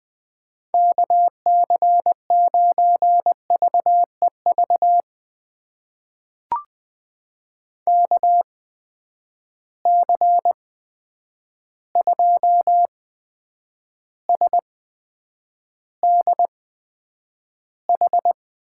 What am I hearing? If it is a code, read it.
Morse code, heard as a keyed tone, about 20 wpm, spaced as for 9 wpm, 700 Hz. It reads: KC9VEV KC2SDH
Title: USA Call Signs - 6 Characters - 8x Character Spacing - 20wpm